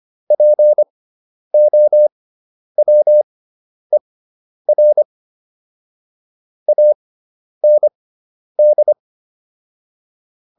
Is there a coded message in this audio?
Yes: Morse code from 0.3 to 8.9 s, heard as a keyed tone, about 25 wpm, spaced as for 10 wpm, 600 Hz. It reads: POWER AND